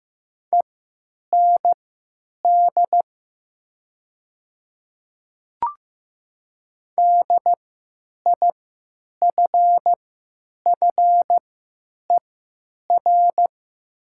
Morse code, heard as a keyed tone, about 15 wpm, spaced as for 9 wpm, 700 Hz. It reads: END DIFFER
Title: 3x Character Spacing - Top 300 Words - 15wpm